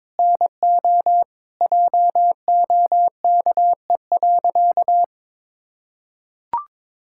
Morse code, heard as a keyed tone, about 22 wpm, 700 Hz. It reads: NO JOKE.
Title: Pangram Challenge v2 - 22wpm